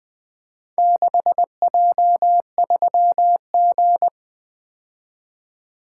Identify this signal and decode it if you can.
Morse code, heard as a keyed tone, about 20 wpm, 700 Hz. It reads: 6J3G
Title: Four Letter-Number Combinations - 20wpm